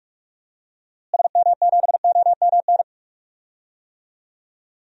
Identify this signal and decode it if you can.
Morse code, heard as a keyed tone, about 46 wpm, 700 Hz. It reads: SM7OMN
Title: DXCC-100 Call Signs - 6 Characters - Repeated 3x - 45wpm